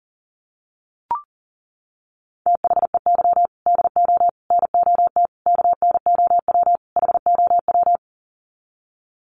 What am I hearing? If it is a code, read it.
Morse code, heard as a keyed tone, about 40 wpm, 700 Hz. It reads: THEY DO NOT KNOW HOW
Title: Sets of 5 Words Encore2 40wpm